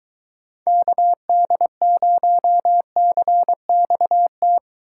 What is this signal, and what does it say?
Morse code, heard as a keyed tone, about 23 wpm, 700 Hz. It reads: KD0CXT